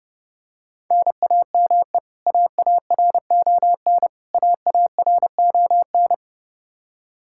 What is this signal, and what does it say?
Morse code, heard as a keyed tone, about 30 wpm, 700 Hz. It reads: NAME AARON AARON
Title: Practice QSO - Exchange 1 - Beginning - 30wpm